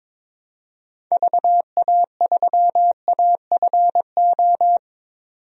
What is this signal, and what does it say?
Morse code, heard as a keyed tone, about 22 wpm, 700 Hz. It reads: VA3AFO
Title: DXCC-100 Call Signs - 6 Characters - 22wpm